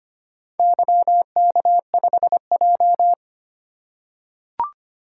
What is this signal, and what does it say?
Morse code, heard as a keyed tone, about 25 wpm, 700 Hz. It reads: YK5J
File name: practice / 25wpm / Four Letter-Number Combinations - 25wpm